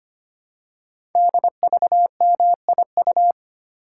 Morse code, heard as a keyed tone, about 25 wpm, 700 Hz. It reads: DVMIU
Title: Code Groups - 5 Letters - 25wpm